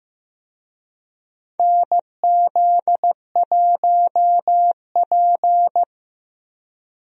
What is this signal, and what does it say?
Morse code, heard as a keyed tone, about 15 wpm, 700 Hz. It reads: NZ1P
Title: USA Call Signs - 4 Characters - Repeated 2x - 15wpm